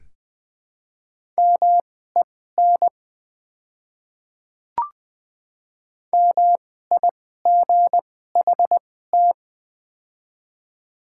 Morse code, heard as a keyed tone, about 20 wpm, spaced as for 14 wpm, 700 Hz. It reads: MEN MIGHT
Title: 2x Character Spacing - Top 300 Words - 20wpm